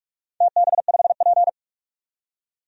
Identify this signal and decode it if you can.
Morse code, heard as a keyed tone, about 47 wpm, 700 Hz. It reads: TB5P
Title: DXCC-100 Call Signs - 4 Characters - 45wpm